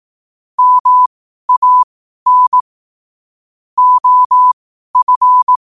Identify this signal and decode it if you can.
Morse code, heard as a keyed tone, about 18 wpm, spaced as for 12 wpm, 1000 Hz. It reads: MAN OF
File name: qotdfiles / 12wpm